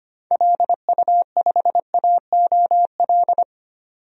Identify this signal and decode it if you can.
Morse code, heard as a keyed tone, about 25 wpm, 700 Hz. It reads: LU5AOL
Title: DXCC-100 Call Signs - 6 Characters - 25wpm